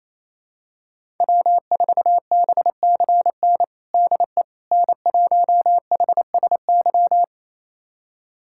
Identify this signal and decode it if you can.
Morse code, heard as a keyed tone, about 28 wpm, 700 Hz. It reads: W4BCN DE N1HSY
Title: USA Callsign DE Callsign - 5 Characters - 28wpm